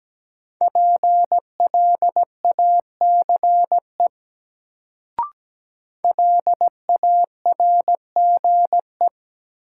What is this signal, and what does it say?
Morse code, heard as a keyed tone, about 17 wpm, 700 Hz. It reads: PLACE LARGE
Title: Sets of 4 Words - New Word Drill - 17wpm